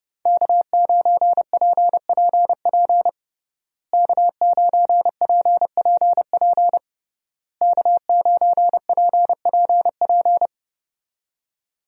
Morse code, heard as a keyed tone, about 30 wpm, 700 Hz. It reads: K9PPP K9PPP K9PPP
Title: USA Call Signs - 5 Characters - Repeated 3x - 30wpm